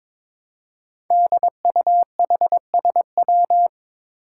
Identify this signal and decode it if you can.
Morse code, heard as a keyed tone, about 22 wpm, 700 Hz. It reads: DUHSW
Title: Five Challenging Letter-Number Combinations 22wpm